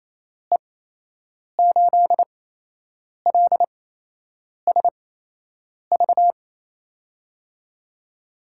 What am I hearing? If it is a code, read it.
Morse code, heard as a keyed tone, about 28 wpm, spaced as for 8 wpm, 700 Hz. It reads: E8LSV
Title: USA Call Signs - 6 Characters - 8x Character Spacing - 28wpm